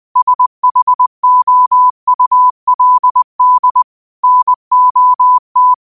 Morse code, heard as a keyed tone, about 20 wpm, 1000 Hz. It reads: SHOULD NOT
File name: qotdfiles / 20wpm